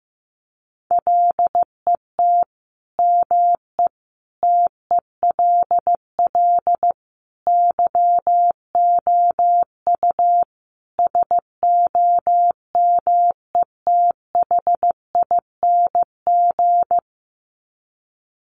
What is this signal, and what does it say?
Morse code, heard as a keyed tone, about 15 wpm, 700 Hz. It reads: LET ME TELL YOU SOMETHING
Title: Sets of 5 Words 15wpm